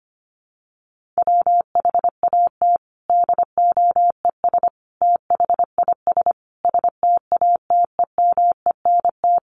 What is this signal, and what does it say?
Morse code, heard as a keyed tone, about 25 wpm, 700 Hz. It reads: WHAT DOES THIS STATEMENT
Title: Sentences from Top 1200 Words 25wpm